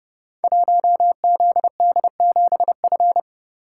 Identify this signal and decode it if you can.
Morse code, heard as a keyed tone, about 30 wpm, 700 Hz. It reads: 1ZD7F
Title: Code Groups - 5 Letter:Numbers - 30wpm